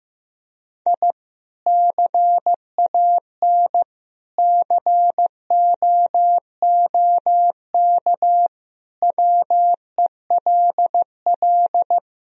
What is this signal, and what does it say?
Morse code, heard as a keyed tone, about 15 wpm, 700 Hz. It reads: I CAN COOK WELL
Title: Sentences from Top 1000 Words - Review Part 4 - 15wpm